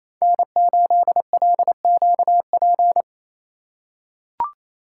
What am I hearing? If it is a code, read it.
Morse code, heard as a keyed tone, about 28 wpm, 700 Hz. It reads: N8LQP
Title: USA Call Signs - 5 Characters - 8x Character Spacing - 28wpm